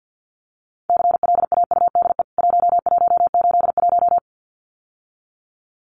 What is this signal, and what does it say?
Morse code, heard as a keyed tone, about 50 wpm, 700 Hz. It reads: CLAUDE 1181
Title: CWT Exchange 50wpm